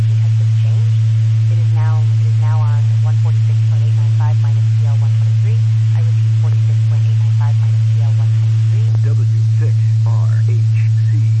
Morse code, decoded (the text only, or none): none